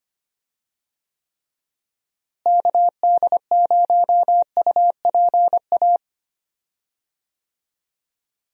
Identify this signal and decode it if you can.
Morse code, heard as a keyed tone, about 25 wpm, 700 Hz. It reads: KD0UPA
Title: USA Call Signs - 6 Characters - Repeated 2x - 25wpm